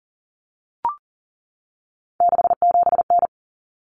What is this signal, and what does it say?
Morse code, heard as a keyed tone, about 40 wpm, 700 Hz. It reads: 67N
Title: Common Cut Numbers - 3-Digit 40wpm